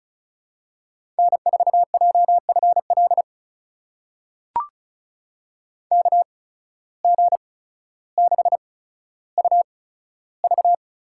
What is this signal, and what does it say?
Morse code, heard as a keyed tone, about 36 wpm, spaced as for 15 wpm, 700 Hz. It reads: N4JFL KG6UV
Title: USA Call Signs - 5 Characters - 8x Character Spacing - 35wpm